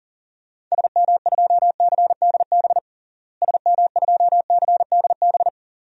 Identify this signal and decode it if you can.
Morse code, heard as a keyed tone, about 37 wpm, 700 Hz. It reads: SM2CDB SM2CDB
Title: DXCC-100 Call Signs - 6 Characters - Repeated 2x - 40wpm